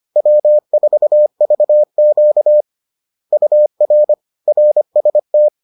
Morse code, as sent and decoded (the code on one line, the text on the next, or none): .-- ....- ...- --.- / ..- .-. / .-. ... -
W4VQ UR RST